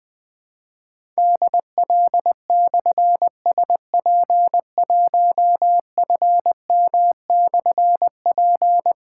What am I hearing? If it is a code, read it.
Morse code, heard as a keyed tone, about 20 wpm, 700 Hz. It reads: DL/SP1FM/P
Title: Difficult Call Signs 20wpm